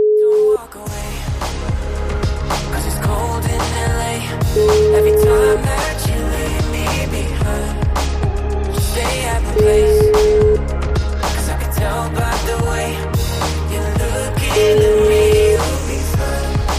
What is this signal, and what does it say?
Ringback tone (Europe, 425 Hz), with noise.